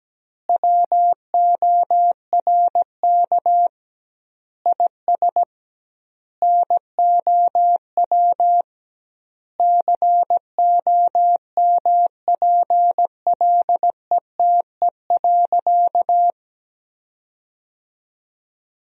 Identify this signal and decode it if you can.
Morse code, heard as a keyed tone, about 17 wpm, 700 Hz. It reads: WORK IS NOW COMPLETE.
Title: 2x Word-Spacing - Sentences from Top 400 Words - 17wpm